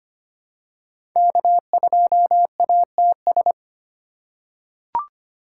Morse code, heard as a keyed tone, about 25 wpm, 700 Hz. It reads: K2ATS
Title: USA Call Signs - 5 Characters - 2x Character Spacing - 25wpm